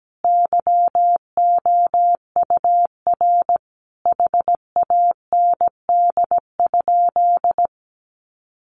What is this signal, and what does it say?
Morse code, heard as a keyed tone, about 17 wpm, 700 Hz. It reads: YOUR HAND?
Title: Sentences from Top 200 Words 17wpm